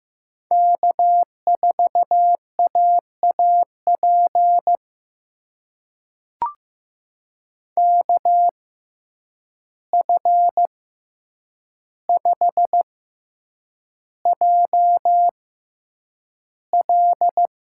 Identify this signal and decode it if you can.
Morse code, heard as a keyed tone, about 15 wpm, spaced as for 8 wpm, 700 Hz. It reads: K4AAP KF5JL
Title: USA Call Signs - 5 Characters - 6x Character Spacing - 15wpm